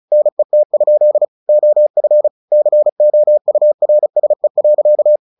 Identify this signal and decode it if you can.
Morse code, heard as a keyed tone, about 34 wpm, 600 Hz. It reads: NET? OF COURSE.
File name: the old man and the sea / the old man and the sea 1 35wpm